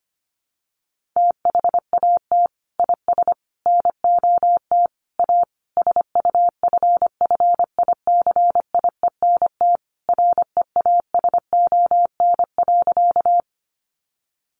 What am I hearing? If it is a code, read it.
Morse code, heard as a keyed tone, about 25 wpm, 700 Hz. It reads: THAT IS NOT A SUFFICIENT REASON.
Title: Sentences from Top 1400 Words 25wpm